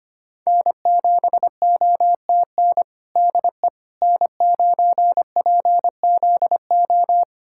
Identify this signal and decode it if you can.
Morse code, heard as a keyed tone, about 25 wpm, 700 Hz. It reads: N7OTN DE N9PZO
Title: USA Callsign DE Callsign - 5 Characters - 25wpm